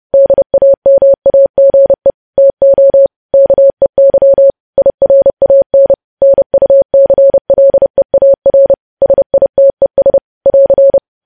Morse code, heard as a keyed tone, about 30 wpm, 550 Hz. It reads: DAMAGE TO KEY IRAN NUCLEAR SITES <AR>
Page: news headlines in morse code at 30 wpm